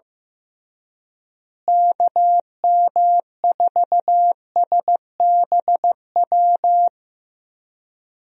Morse code, heard as a keyed tone, about 15 wpm, 700 Hz. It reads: KM4SBW
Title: USA Call Signs - 6 Characters - Repeated 3x - 15wpm